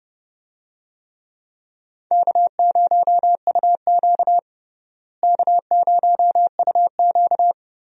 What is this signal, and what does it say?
Morse code, heard as a keyed tone, about 31 wpm, 700 Hz. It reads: K0UQ K0UQ
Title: USA Call Signs - 4 Characters - Repeated 3x - 30wpm